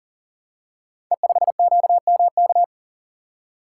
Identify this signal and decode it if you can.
Morse code, heard as a keyed tone, about 39 wpm, 700 Hz. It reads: E5QMK
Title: Code Groups - 5 Letter:Numbers - 40wpm